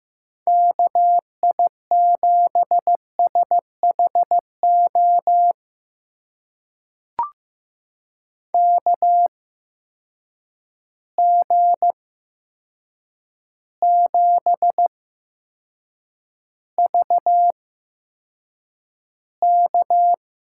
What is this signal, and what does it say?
Morse code, heard as a keyed tone, about 15 wpm, spaced as for 7 wpm, 700 Hz. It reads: KI7SHO KG7VK
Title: USA Call Signs - 6 Characters - 8x Character Spacing - 15wpm